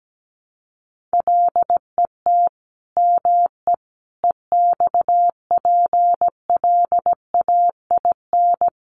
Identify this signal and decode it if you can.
Morse code, heard as a keyed tone, about 17 wpm, 700 Hz. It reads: LET ME EXPLAIN
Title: Sentences from Top 1300 Words 17wpm